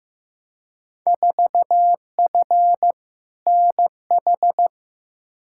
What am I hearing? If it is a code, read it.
Morse code, heard as a keyed tone, about 15 wpm, 700 Hz. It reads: 4F NH
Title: Field Day Exchanges Encore - 15wpm